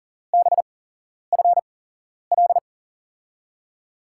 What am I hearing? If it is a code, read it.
Morse code, heard as a keyed tone, about 41 wpm, spaced as for 11 wpm, 700 Hz. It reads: BFL
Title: USA Call Signs - 6 Characters - 8x Character Spacing - 40wpm